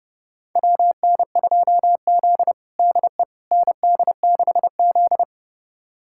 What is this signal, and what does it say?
Morse code, heard as a keyed tone, about 30 wpm, 700 Hz. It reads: WN2Z DE ND6Z
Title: USA Callsign DE Callsign - 4 Characters - 30wpm